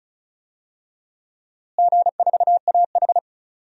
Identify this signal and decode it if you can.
Morse code, heard as a keyed tone, about 36 wpm, 700 Hz. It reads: G4AH
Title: DXCC-100 Call Signs - 4 Characters - Repeated 3x - 35wpm